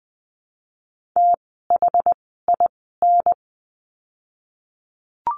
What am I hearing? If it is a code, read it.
Morse code, heard as a keyed tone, about 20 wpm, spaced as for 15 wpm, 700 Hz. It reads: THIN
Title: 068 Lesson 12f - Letter d - Words 20wpm